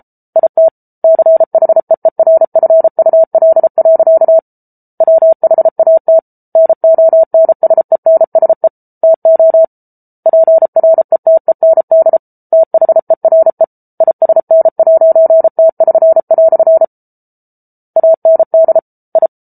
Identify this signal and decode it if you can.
Morse code, heard as a keyed tone, about 34 wpm, 650 Hz. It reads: IT CHEERFUL. WHAT NONSENSE TO PRETEND THERE ISN'T<SN>" AND I